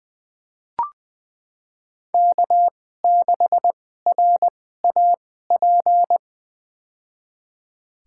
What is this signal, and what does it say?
Morse code, heard as a keyed tone, about 20 wpm, spaced as for 15 wpm, 700 Hz. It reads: K6RAP
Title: USA Call Signs - 5 Characters - 2x Character Spacing - 20wpm